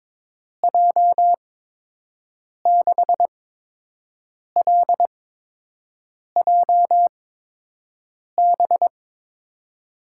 Morse code, heard as a keyed tone, about 22 wpm, 700 Hz. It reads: J 6 L J B
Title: USA Call Signs - 6 Characters - 8x Character Spacing - 22wpm